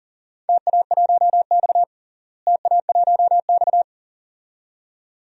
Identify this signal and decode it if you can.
Morse code, heard as a keyed tone, about 40 wpm, 700 Hz. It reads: TA1X TA1X
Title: DXCC-100 Call Signs - 4 Characters - Repeated 2x - 40wpm